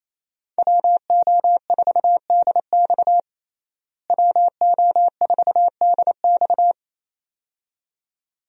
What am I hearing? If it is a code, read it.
Morse code, heard as a keyed tone, about 28 wpm, 700 Hz. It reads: WO4DX WO4DX
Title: USA Call Signs - 5 Characters - Repeated 2x - 28wpm